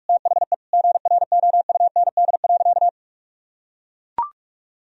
Morse code, heard as a keyed tone, about 41 wpm, 700 Hz. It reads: THE GROUND.
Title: Sentences from Top 1000 Words - Review Part 2 - 45wpm